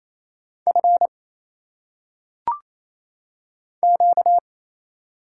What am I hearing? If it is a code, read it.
Morse code, heard as a keyed tone, about 27 wpm, 700 Hz. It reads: F Q
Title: Single Letters - Phonetic Pronunciation 28wpm